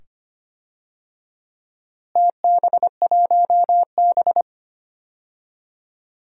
Morse code, heard as a keyed tone, about 25 wpm, 700 Hz. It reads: TB1B
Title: DXCC-100 Call Signs - 4 Characters - Repeated 2x - 25wpm